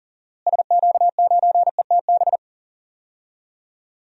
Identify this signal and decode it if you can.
Morse code, heard as a keyed tone, about 40 wpm, 700 Hz. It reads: SQ9ETB
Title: DXCC-100 Call Signs - 6 Characters - Repeated 2x - 40wpm